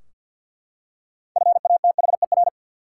Morse code, heard as a keyed tone, about 53 wpm, 700 Hz. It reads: FATHER
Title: Six Letter Words - 50wpm